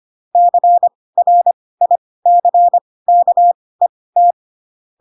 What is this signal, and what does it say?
Morse code, heard as a keyed tone, about 25 wpm, spaced as for 18 wpm, 700 Hz. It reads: CRICKET
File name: Pinocchio / Pinocchio ch4 12 25 ik1hge